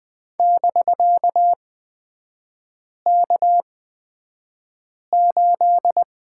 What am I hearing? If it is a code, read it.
Morse code, heard as a keyed tone, about 20 wpm, spaced as for 5 wpm, 700 Hz. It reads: <BK>K8
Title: 250 Lesson 40e - BK - Three Characters 20wpm